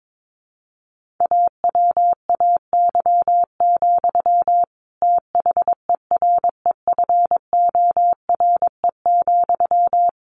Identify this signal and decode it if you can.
Morse code, heard as a keyed tone, about 22 wpm, 700 Hz. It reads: AWAY, THEREFORE,
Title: Sentences from Top 1500 Words 22wpm